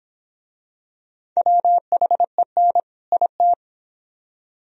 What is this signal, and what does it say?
Morse code, heard as a keyed tone, about 21 wpm, 700 Hz. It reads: WHEN IT